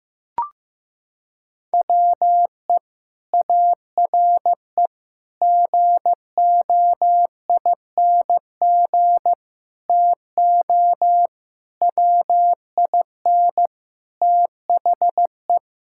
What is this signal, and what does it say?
Morse code, heard as a keyed tone, about 15 wpm, 700 Hz. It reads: WE ARE GOING TO WIN THE
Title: Sentences from Top 1600 Words 15wpm